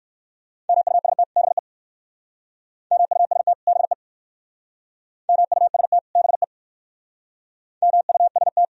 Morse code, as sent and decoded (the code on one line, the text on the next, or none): -- ..- ... - / -... / -- ..- ... - / -... / -- ..- ... - / -... / -- ..- ... -
MUST BE MUST BE MUST BE MUST